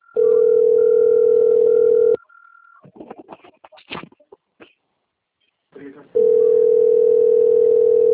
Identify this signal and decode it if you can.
Ringback tone (North America), 440 + 480 Hz, with speech.
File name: rg-1600-unknown- - - .17306